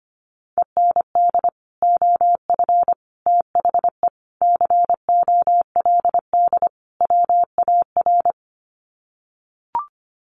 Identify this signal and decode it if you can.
Morse code, heard as a keyed tone, about 25 wpm, 700 Hz. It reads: END OF THE COLD WAR
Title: Sets of 5 Words Encore 25wpm